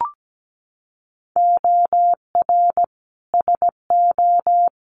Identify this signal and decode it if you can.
Morse code, heard as a keyed tone, about 17 wpm, 700 Hz. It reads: OR SO
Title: Sentences from Top 100 Words 17wpm